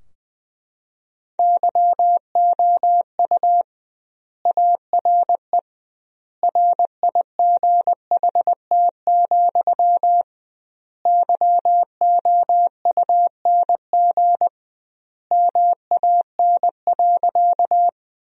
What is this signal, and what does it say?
Morse code, heard as a keyed tone, about 21 wpm, 700 Hz. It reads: YOU ARE RIGHT, YOUNG MAN.
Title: 2x Word-Spacing - Sentences from Top 400 Words - 20wpm